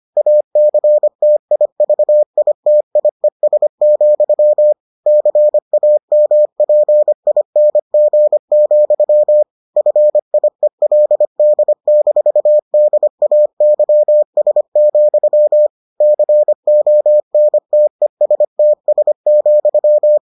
Morse code, heard as a keyed tone, about 25 wpm, 600 Hz. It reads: ACTIVITIES, CAMPING, FIELD-DAYS, CONTESTS,